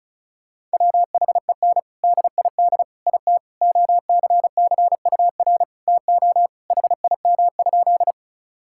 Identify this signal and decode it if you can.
Morse code, heard as a keyed tone, about 34 wpm, 700 Hz. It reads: WHEN DID IT OCCUR TO HIM?
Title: Sentences from Top 1000 Words - Review Part 6 - 35wpm